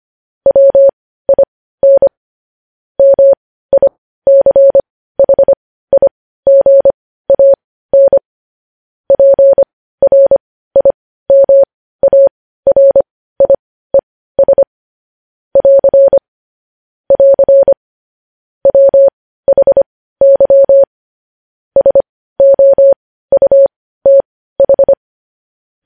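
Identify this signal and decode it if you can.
Morse code, heard as a keyed tone, about 25 wpm, spaced as for 15 wpm, 550 Hz. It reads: WIN MICHIGAN PRIMARIES <AR> <AR> WHY SOUTH
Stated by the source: News headlines sent as Morse practice at 15.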